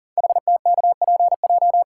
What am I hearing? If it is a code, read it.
Morse code, heard as a keyed tone, about 40 wpm, 700 Hz. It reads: HTKPJ
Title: Code Groups - 5 Letter:Numbers - 40wpm